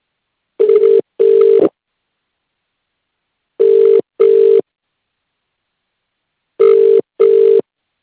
Ringback tone (UK), 400 + 450 Hz.